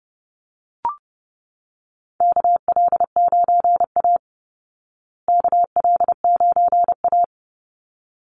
Morse code, heard as a keyed tone, about 28 wpm, 700 Hz. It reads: KL9A KL9A
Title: Call Signs Repeated 3x - 30wpm